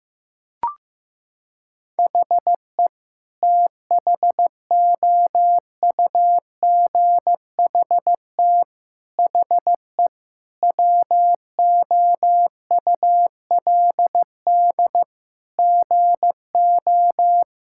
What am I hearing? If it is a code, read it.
Morse code, heard as a keyed tone, about 15 wpm, 700 Hz. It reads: HE THOUGHT HE WOULD GO
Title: Sentences from Top 1000 Words - Review Part 6 - 15wpm